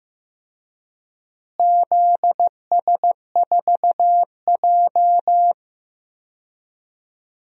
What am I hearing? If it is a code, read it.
Morse code, heard as a keyed tone, about 15 wpm, 700 Hz. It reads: ZS4J
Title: DXCC-100 Call Signs - 4 Characters - Repeated 3x - 15wpm